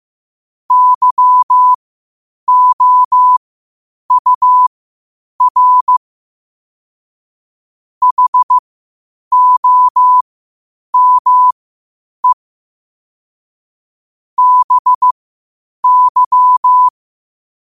Morse code, heard as a keyed tone, about 15 wpm, spaced as for 8 wpm, 1000 Hz. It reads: YOUR HOME BY
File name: qotdfiles / 8wpm